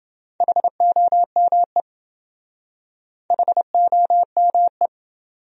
Morse code, heard as a keyed tone, about 26 wpm, 700 Hz. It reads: HOME HOME